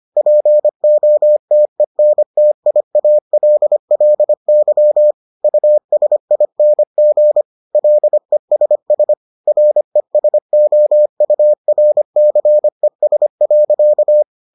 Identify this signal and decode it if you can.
Morse code, heard as a keyed tone, about 25 wpm, 600 Hz. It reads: POTENTIALLY USING LESS RESOURCES.